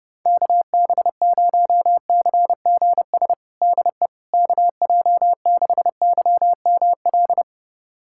Morse code, heard as a keyed tone, about 32 wpm, 700 Hz. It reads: KB0CGS DE KJ6YML
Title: USA Callsign DE Callsign - 6 Characters - 30wpm